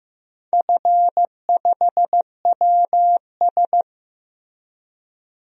Morse code, heard as a keyed tone, about 15 wpm, 700 Hz. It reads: F5WS
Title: DXCC-100 Call Signs - 4 Characters - Repeated 3x - 15wpm